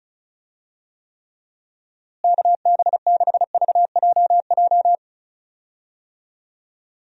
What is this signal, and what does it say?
Morse code, heard as a keyed tone, about 36 wpm, 700 Hz. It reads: KB6VJJ
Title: USA Call Signs - 6 Characters - Repeated 3x - 35wpm